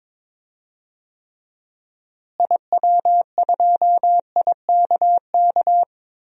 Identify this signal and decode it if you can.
Morse code, heard as a keyed tone, about 22 wpm, 700 Hz. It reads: IW2IKK
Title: DXCC-100 Call Signs - 6 Characters - Repeated 2x - 22wpm